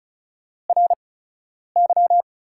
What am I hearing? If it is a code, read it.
Morse code, heard as a keyed tone, about 36 wpm, spaced as for 10 wpm, 700 Hz. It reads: RY
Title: 8x Character Spacing - Top 300 Words - 35wpm